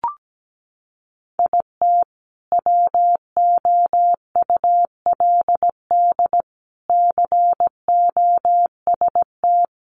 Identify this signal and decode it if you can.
Morse code, heard as a keyed tone, about 17 wpm, 700 Hz. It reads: IT WOULD COST
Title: Sentences from Top 800 Words 17wpm